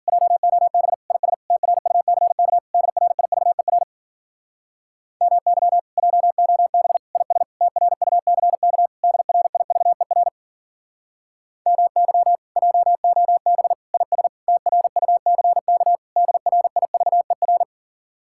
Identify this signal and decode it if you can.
Morse code, heard as a keyed tone, about 55 wpm, 700 Hz. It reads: JOB IS TRUCK DRIVER MY JOB IS TRUCK DRIVER MY JOB IS TRUCK DRIVER